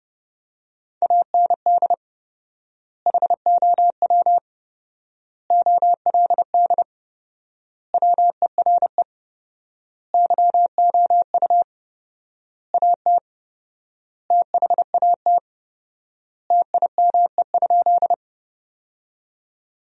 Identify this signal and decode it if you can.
Morse code, heard as a keyed tone, about 32 wpm, 700 Hz. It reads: AND HOW OLD WERE YOU AT THAT TIME?
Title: 4x Word-Spacing - Sentences from Top 200 Words - 30wpm